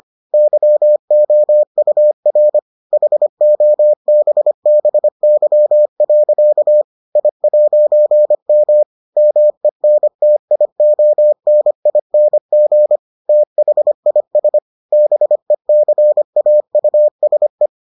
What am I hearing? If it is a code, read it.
Morse code, heard as a keyed tone, about 25 wpm, 600 Hz. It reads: YOUR HOBBY. I'M MENTIONING THIS BECAUSE